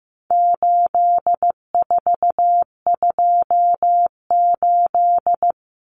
Morse code, heard as a keyed tone, about 15 wpm, 700 Hz. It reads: 8428
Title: Numbers - 4-Digit 15wpm